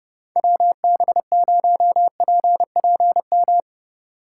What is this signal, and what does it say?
Morse code, heard as a keyed tone, about 29 wpm, 700 Hz. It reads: WB0PPM